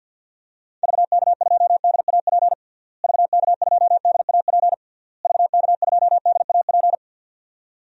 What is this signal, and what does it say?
Morse code, heard as a keyed tone, about 46 wpm, 700 Hz. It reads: VK2DAP VK2DAP VK2DAP
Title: DXCC-100 Call Signs - 6 Characters - Repeated 3x - 50wpm